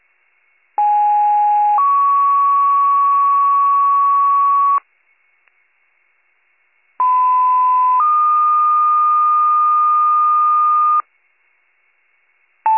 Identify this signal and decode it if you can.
Two-tone page, heard as a one-second tone, then a three-second tone, about 830 then 1120 Hz, about 1010 then 1220 Hz. It is heard from 0.8 to 11.0 s.